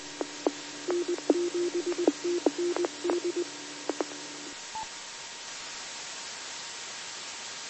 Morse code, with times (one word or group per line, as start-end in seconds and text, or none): none